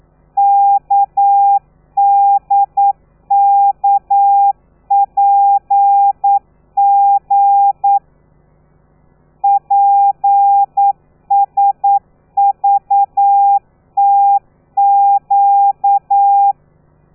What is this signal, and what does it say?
Morse code, heard as a keyed tone, about 9 wpm, 800 Hz. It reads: KDKPG PSVTQ
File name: telegrafia / i2tfj / Lez13 40